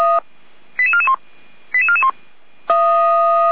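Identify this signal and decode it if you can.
Five-tone selective call (ZVEI), heard as quick runs of five tones, tones of about 80 ms. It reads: 88441 88441
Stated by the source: ZVEI five-tone call received on an SDR.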